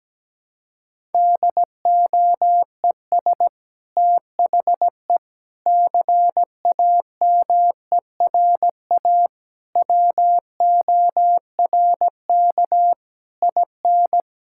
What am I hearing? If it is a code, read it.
Morse code, heard as a keyed tone, about 17 wpm, 700 Hz. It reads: DOES THE CAMERA WORK IN
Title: Sentences from Top 1000 Words - Review Part 5 - 17wpm